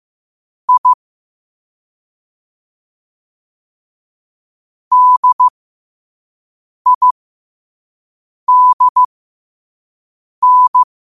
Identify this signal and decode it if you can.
Morse code, heard as a keyed tone, about 15 wpm, spaced as for 5 wpm, 1000 Hz. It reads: I DIDN
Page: qotd-at-5-wpm